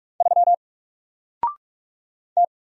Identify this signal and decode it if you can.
Morse code, heard as a keyed tone, about 46 wpm, 700 Hz. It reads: 3 T
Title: Single Letter-Number 45wpm